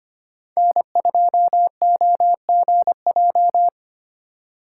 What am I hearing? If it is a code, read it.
Morse code, heard as a keyed tone, about 25 wpm, 700 Hz. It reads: N2OGJ
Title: USA Call Signs - 5 Characters - 2x Character Spacing - 25wpm